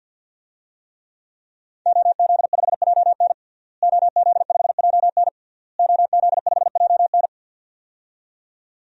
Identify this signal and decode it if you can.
Morse code, heard as a keyed tone, about 49 wpm, 700 Hz. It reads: OZ5JN OZ5JN OZ5JN
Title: DXCC-100 Call Signs - 5 Characters - Repeated 3x - 50wpm